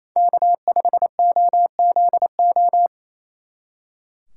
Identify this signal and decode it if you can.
Morse code, heard as a keyed tone, about 26 wpm, 700 Hz. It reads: K5OZO
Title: Speed Racing - USA 1x3 Call Signs 28wpm